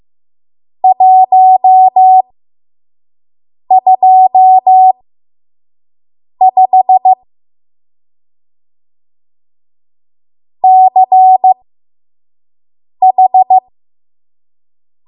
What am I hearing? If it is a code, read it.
Morse code, heard as a keyed tone, about 15 wpm, spaced as for 5 wpm, 750 Hz. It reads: 125 CH
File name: Morse / 05WPM